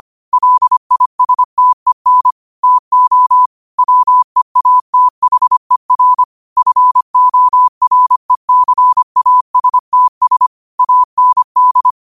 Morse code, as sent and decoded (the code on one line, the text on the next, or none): .-.. .. ... - . -. / - --- / .-- . .- - .... .-. / ..-. --- .-. -.-. .- ... - ... / .- -. -..
LISTEN TO WEATHER FORECASTS AND